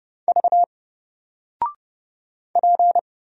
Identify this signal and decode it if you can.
Morse code, heard as a keyed tone, about 31 wpm, 700 Hz. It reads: V P
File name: Single-Letter-Number-30wpm